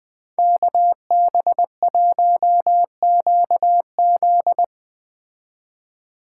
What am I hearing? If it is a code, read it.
Morse code, heard as a keyed tone, about 22 wpm, 700 Hz. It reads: KB1QZ